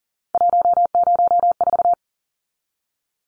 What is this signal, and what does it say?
Morse code, heard as a keyed tone, about 40 wpm, 700 Hz. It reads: 104
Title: Numbers - 3-Digit 40wpm